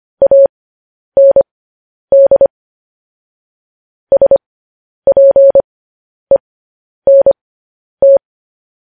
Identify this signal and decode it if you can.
Morse code, heard as a keyed tone, about 25 wpm, spaced as for 10 wpm, 550 Hz. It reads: AND SPENT
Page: News Headlines in Morse Code at 10 WPM